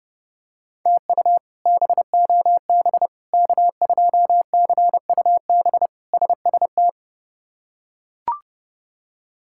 Morse code, heard as a keyed tone, about 30 wpm, 700 Hz. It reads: TU BOB K2CUB SST
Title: K1USN SST QSO Practice - 30wpm